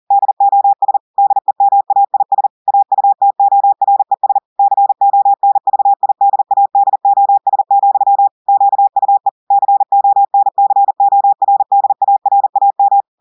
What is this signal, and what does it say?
Morse code, heard as a keyed tone, about 40 wpm, 800 Hz. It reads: DOS DEMAIS AUTORES CONVIDADOS, QUE CONCORDARAM
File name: audios / portugais / portugais texte en clair 1 vitesse 40 mots minutes